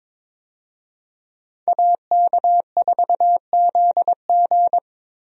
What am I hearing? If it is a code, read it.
Morse code, heard as a keyed tone, about 21 wpm, 700 Hz. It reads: AK4ZG
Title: USA Call Signs - 5 Characters - Repeated 2x - 22wpm